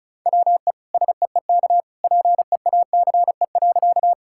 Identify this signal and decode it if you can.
Morse code, heard as a keyed tone, about 36 wpm, 700 Hz. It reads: WE SEEK PEACE.
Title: Sentences from Top 1600 Words 35wpm